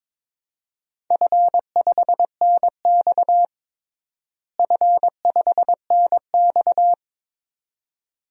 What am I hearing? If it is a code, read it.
Morse code, heard as a keyed tone, about 22 wpm, 700 Hz. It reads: F5NX F5NX
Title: DXCC-100 Call Signs - 4 Characters - Repeated 2x - 22wpm